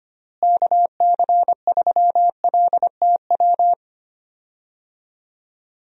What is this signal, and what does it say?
Morse code, heard as a keyed tone, about 25 wpm, 700 Hz. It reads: KC3LTW
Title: USA Call Signs - 6 Characters - Repeated 3x - 25wpm